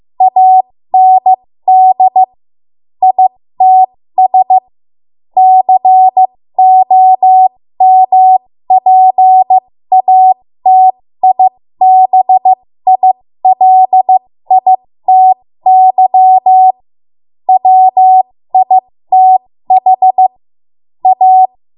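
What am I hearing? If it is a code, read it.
Morse code, heard as a keyed tone, about 15 wpm, spaced as for 13 wpm, 750 Hz. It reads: AND ITS COMPATIBILITY WITH A
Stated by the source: 13 wpm Morse code practice archive transmission.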